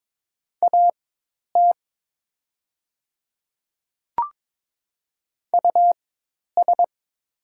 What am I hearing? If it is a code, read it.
Morse code, heard as a keyed tone, about 22 wpm, spaced as for 10 wpm, 700 Hz. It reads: AT US
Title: 4x Character Spacing - Top 300 Words - 22wpm